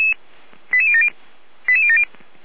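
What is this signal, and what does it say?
Five-tone selective call (ZVEI), heard as quick runs of five tones, tones of about 70 ms. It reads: 88070 88070
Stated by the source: ZVEI five-tone selective call received by SDR.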